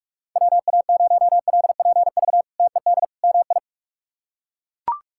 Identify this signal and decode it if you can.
Morse code, heard as a keyed tone, about 41 wpm, 700 Hz. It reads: WA0LPV TED MI
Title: K1USN SST QSO Practice - 45wpm